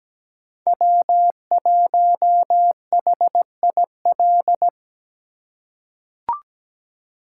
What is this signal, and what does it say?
Morse code, heard as a keyed tone, about 17 wpm, 700 Hz. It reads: W1HIL
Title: USA Call Signs - 5 Characters - 2x Character Spacing - 17wpm